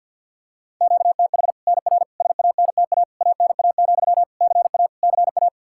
Morse code, heard as a keyed tone, about 49 wpm, 700 Hz. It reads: QTH NR SANTA ANA, CA CA
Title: Practice QSO - Exchange 1 - Beginning - 50wpm